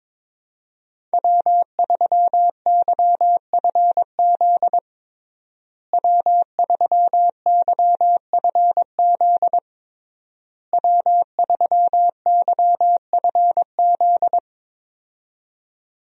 Morse code, heard as a keyed tone, about 21 wpm, 700 Hz. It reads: W3YFZ W3YFZ W3YFZ
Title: USA Call Signs - 5 Characters - Repeated 3x - 22wpm